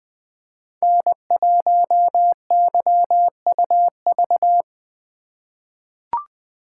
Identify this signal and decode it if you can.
Morse code, heard as a keyed tone, about 20 wpm, 700 Hz. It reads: N1YUV
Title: USA Call Signs - 5 Characters - 4x Character Spacing - 20wpm